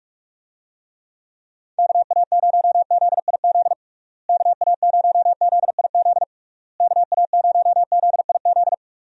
Morse code, heard as a keyed tone, about 46 wpm, 700 Hz. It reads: KA0ZIZ KA0ZIZ KA0ZIZ
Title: USA Call Signs - 6 Characters - Repeated 3x - 45wpm